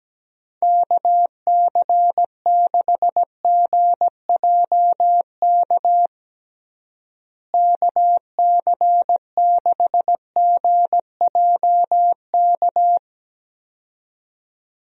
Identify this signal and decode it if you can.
Morse code, heard as a keyed tone, about 17 wpm, 700 Hz. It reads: KC6GJK KC6GJK
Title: USA Call Signs - 6 Characters - Repeated 2x - 17wpm